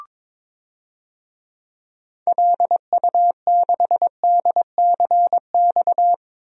Morse code, heard as a keyed tone, about 22 wpm, 700 Hz. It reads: LU6DCX
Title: DXCC-100 Call Signs - 6 Characters - 22wpm